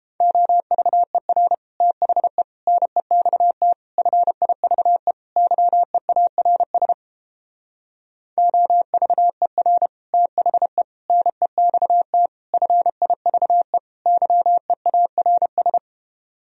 Morse code, heard as a keyed tone, about 29 wpm, 700 Hz. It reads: OVER THE NEXT FIVE YEARS OVER THE NEXT FIVE YEARS